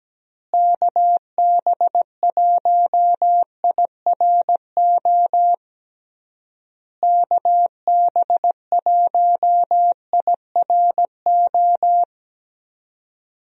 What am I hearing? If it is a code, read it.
Morse code, heard as a keyed tone, about 17 wpm, 700 Hz. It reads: KB1IRO KB1IRO
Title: USA Call Signs - 6 Characters - Repeated 3x - 17wpm